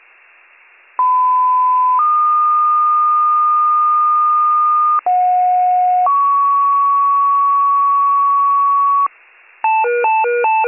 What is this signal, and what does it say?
Two-tone page, heard as a one-second tone, then a three-second tone, about 1010 then 1220 Hz, about 730 then 1120 Hz. It is heard from 1.0 to 9.1 s.